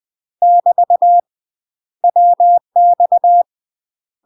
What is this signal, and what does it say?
Morse code, heard as a keyed tone, about 20 wpm, 700 Hz. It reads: <BT> WX
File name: UK MASTER QSOs TRAINING / 20WPM / UK MASTER QSOs 0024